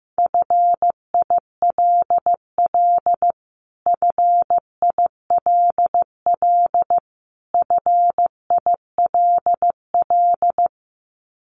Morse code, heard as a keyed tone, about 15 wpm, 700 Hz. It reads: FILL FILL FILL
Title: Top 500 Words Repeated 15wpm